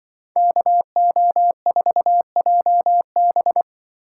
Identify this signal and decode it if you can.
Morse code, heard as a keyed tone, about 24 wpm, 700 Hz. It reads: KO4JB
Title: Speed Racing - USA 2x2 Call Signs 20wpm